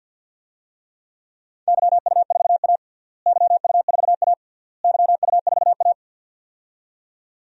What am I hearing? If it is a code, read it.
Morse code, heard as a keyed tone, about 50 wpm, 700 Hz. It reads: YU4A YU4A YU4A
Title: DXCC-100 Call Signs - 4 Characters - Repeated 3x - 50wpm